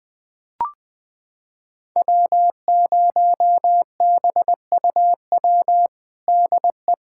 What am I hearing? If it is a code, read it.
Morse code, heard as a keyed tone, about 20 wpm, 700 Hz. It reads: W0BUW DE
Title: USA Callsign DE Callsign - 5 Characters - 20wpm